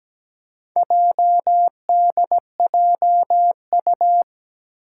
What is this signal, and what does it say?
Morse code, heard as a keyed tone, about 17 wpm, 700 Hz. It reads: JDJU
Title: Four Challenging Letter-Number Combinations 17wpm